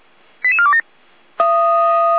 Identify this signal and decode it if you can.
Five-tone selective call (ZVEI), heard as quick runs of five tones, tones of about 70 ms. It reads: 88427